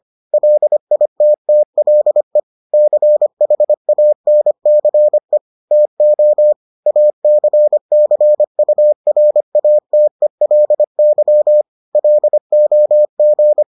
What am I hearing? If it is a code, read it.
Morse code, heard as a keyed tone, about 25 wpm, 600 Hz. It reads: LITTLE CHANCE TO ACCURATELY LOG